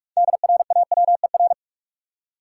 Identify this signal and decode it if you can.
Morse code, heard as a keyed tone, about 46 wpm, 700 Hz. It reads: DRAWER
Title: Six Letter Words - 45wpm